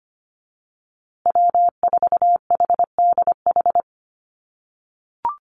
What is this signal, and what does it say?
Morse code, heard as a keyed tone, about 25 wpm, 700 Hz. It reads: W4HDH
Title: USA Call Signs Repeated 2x 25wpm